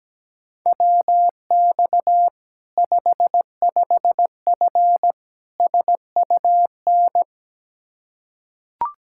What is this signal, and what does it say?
Morse code, heard as a keyed tone, about 17 wpm, 700 Hz. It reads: WX 55F SUN
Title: Practice QSO - Exchange 2 - Info - 17wpm